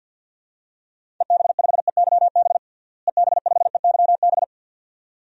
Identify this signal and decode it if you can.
Morse code, heard as a keyed tone, about 52 wpm, 700 Hz. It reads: EB5EYB EB5EYB
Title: DXCC-100 Call Signs - 6 Characters - Repeated 2x - 50wpm